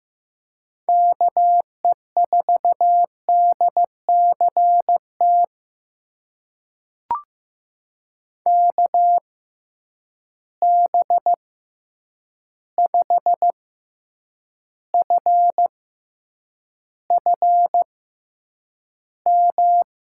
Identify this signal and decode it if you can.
Morse code, heard as a keyed tone, about 16 wpm, spaced as for 8 wpm, 700 Hz. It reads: KE4DCT KB5FFM
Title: USA Call Signs - 6 Characters - 6x Character Spacing - 15wpm